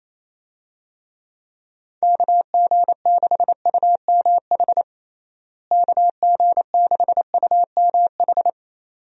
Morse code, heard as a keyed tone, about 28 wpm, 700 Hz. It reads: KG6UMH KG6UMH
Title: USA Call Signs - 6 Characters - Repeated 2x - 28wpm